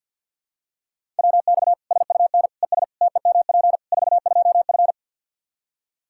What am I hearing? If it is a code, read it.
Morse code, heard as a keyed tone, about 50 wpm, 700 Hz. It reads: WX SUN ES TEMP 42F
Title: K1USN SST QSO Practice 50wpm